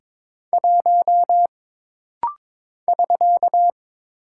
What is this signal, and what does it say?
Morse code, heard as a keyed tone, about 22 wpm, 700 Hz. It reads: 1 <SK>
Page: This rapid-fire timing is geared to push your ICR (Instant Character Recognition) skills.